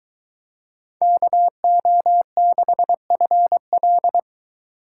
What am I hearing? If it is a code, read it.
Morse code, heard as a keyed tone, about 23 wpm, 700 Hz. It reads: KO6FL